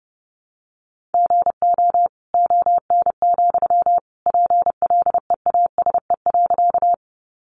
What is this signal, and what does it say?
Morse code, heard as a keyed tone, about 28 wpm, 700 Hz. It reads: GO ON, PLEASE.
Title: Sentences from Top 1500 Words 30wpm